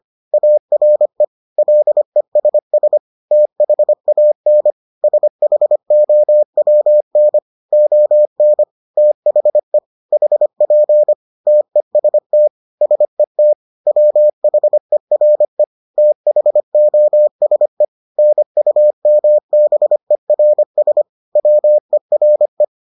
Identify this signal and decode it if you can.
Morse code, heard as a keyed tone, about 25 wpm, 600 Hz. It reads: ARE LESS THAN SHOWN ON THE HP TEST SET WHERE THOSE NUMBERS WERE